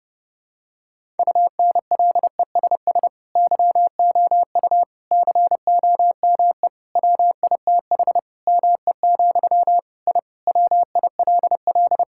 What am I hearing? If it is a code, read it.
Morse code, heard as a keyed tone, about 30 wpm, 700 Hz. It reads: UNLESS YOU COME WITH ME, I WILL
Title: Sentences from Top 1000 Words - Review Part 6 - 30wpm